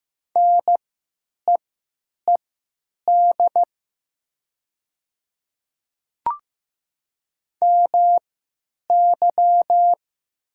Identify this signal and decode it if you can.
Morse code, heard as a keyed tone, about 15 wpm, spaced as for 9 wpm, 700 Hz. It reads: NEED MY
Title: 3x Character Spacing - Top 300 Words - 15wpm